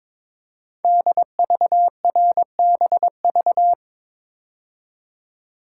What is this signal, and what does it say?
Morse code, heard as a keyed tone, about 22 wpm, 700 Hz. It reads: DVRBV
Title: Code Groups - 5 Letters - 22wpm